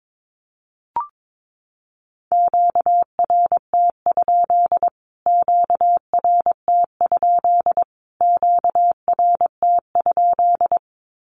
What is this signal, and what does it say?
Morse code, heard as a keyed tone, about 22 wpm, 700 Hz. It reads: QRT? QRT? QRT?
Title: Q-Codes Repeated 3x 22wpm